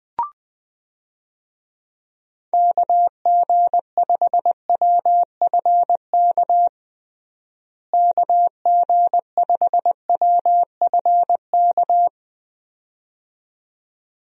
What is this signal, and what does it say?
Morse code, heard as a keyed tone, about 20 wpm, 700 Hz. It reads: KG5WFK KG5WFK
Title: USA Call Signs - 6 Characters - Repeated 2x - 20wpm